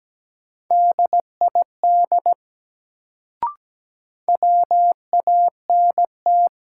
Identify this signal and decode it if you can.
Morse code, heard as a keyed tone, about 17 wpm, 700 Hz. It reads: DID WANT
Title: Sets of 2 Words - New Word Drill - 17wpm